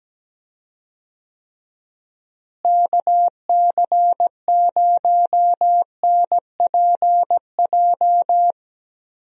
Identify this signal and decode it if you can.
Morse code, heard as a keyed tone, about 17 wpm, 700 Hz. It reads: KC0NPJ
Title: USA Call Signs - 6 Characters - Repeated 3x - 17wpm